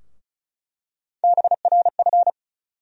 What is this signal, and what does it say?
Morse code, heard as a keyed tone, about 36 wpm, 700 Hz. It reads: BRF
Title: USA Call Sign Postfixes - 3 Letters 35wpm